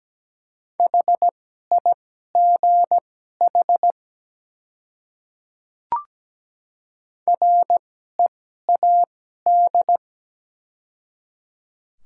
Morse code, heard as a keyed tone, about 17 wpm, spaced as for 12 wpm, 700 Hz. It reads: HIGH READ
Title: 2x Character Spacing - Top 300 Words - 17wpm